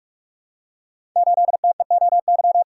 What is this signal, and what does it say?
Morse code, heard as a keyed tone, about 44 wpm, 700 Hz. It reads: 8TEOY
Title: Code Groups - 5 Letter:Numbers - 45wpm